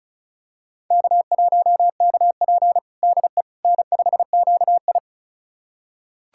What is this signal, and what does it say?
Morse code, heard as a keyed tone, about 36 wpm, 700 Hz. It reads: K1KP DE N5QI
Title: USA Callsign DE Callsign - 4 Characters - 35wpm